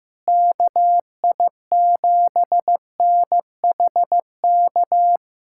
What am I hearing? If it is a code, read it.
Morse code, heard as a keyed tone, about 15 wpm, 700 Hz. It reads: KI7NHK
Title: USA Callsign DE Callsign 15wpm